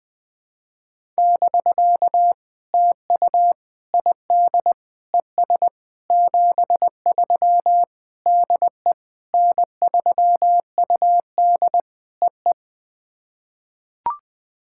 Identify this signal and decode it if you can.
Morse code, heard as a keyed tone, about 20 wpm, 700 Hz. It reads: <BK> TU ID ES 73 DE N3UD EE
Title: POTA QSO Practice - 20wpm